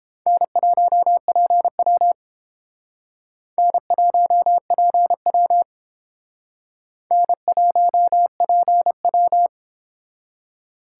Morse code, heard as a keyed tone, about 31 wpm, 700 Hz. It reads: N1PW N1PW N1PW